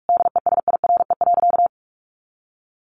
Morse code, heard as a keyed tone, about 44 wpm, 700 Hz. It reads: DESIRE.
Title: Sentences from Top 1300 Words 45wpm